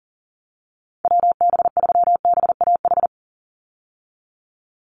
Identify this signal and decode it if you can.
Morse code, heard as a keyed tone, about 40 wpm, 700 Hz. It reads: WB3BAH
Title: USA Call Signs 40wpm